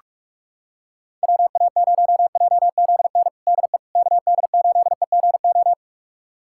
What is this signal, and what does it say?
Morse code, heard as a keyed tone, about 46 wpm, 700 Hz. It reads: WA0JZN DE KD8EGO
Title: USA Callsign DE Callsign 45wpm